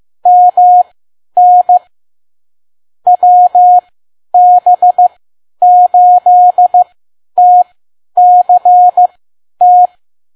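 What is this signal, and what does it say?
Morse code, heard as a keyed tone, about 15 wpm, spaced as for 10 wpm, 720 Hz. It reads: MN WB8TCT
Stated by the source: Morse code practice at 10 WPM.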